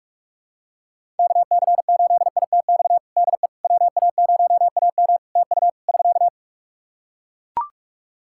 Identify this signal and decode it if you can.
Morse code, heard as a keyed tone, about 44 wpm, 700 Hz. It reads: KC8ITX DE WA0AM TU <SK>
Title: Practice QSO - Exchange 4 - Ending - 45wpm